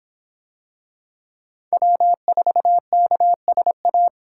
Morse code, heard as a keyed tone, about 23 wpm, 700 Hz. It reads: W4KSA